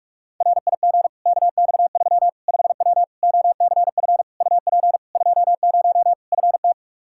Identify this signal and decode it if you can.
Morse code, heard as a keyed tone, about 41 wpm, 700 Hz. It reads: RIG KX3 5W OCF UP 20 FT
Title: Practice QSO - Exchange 2 - Info - 45wpm